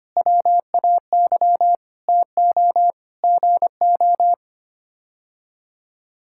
Morse code, heard as a keyed tone, about 25 wpm, 700 Hz. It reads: WAY TO GO
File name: Sets of 9 Words - 25wpm